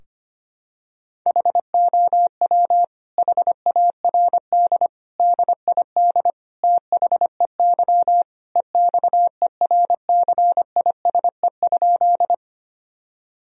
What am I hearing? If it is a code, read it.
Morse code, heard as a keyed tone, about 25 wpm, 700 Hz. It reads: HOW HARD DID THEY EXERCISE?
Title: Sentences from Top 1000 Words - Review Part 4 - 25wpm